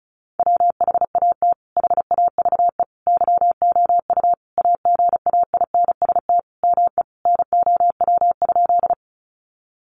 Morse code, heard as a keyed tone, about 34 wpm, 700 Hz. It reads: WHAT HAVE YOU AGAINST ME NOW?
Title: Sentences from Top 400 Words 35wpm